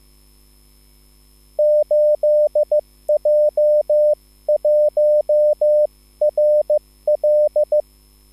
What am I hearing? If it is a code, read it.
Morse code, heard as a keyed tone, about 15 wpm, spaced as for 13 wpm, 600 Hz. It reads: 8J1RL
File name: QSL / CW lesson